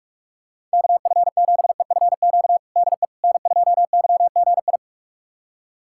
Morse code, heard as a keyed tone, about 47 wpm, 700 Hz. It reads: KF7EFQ DE N2YGI
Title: USA Callsign DE Callsign 45wpm